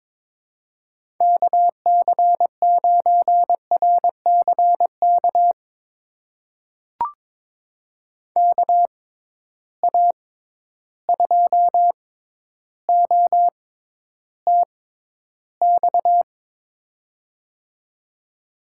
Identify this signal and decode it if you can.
Morse code, heard as a keyed tone, about 22 wpm, spaced as for 11 wpm, 700 Hz. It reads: KC9RCK KA2OTX
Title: USA Call Signs - 6 Characters - 6x Character Spacing - 22wpm